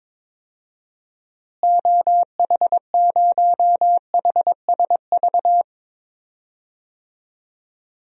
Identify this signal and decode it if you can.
Morse code, heard as a keyed tone, about 22 wpm, 700 Hz. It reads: OH0HSV
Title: DXCC-100 Call Signs - 6 Characters - Repeated 2x - 22wpm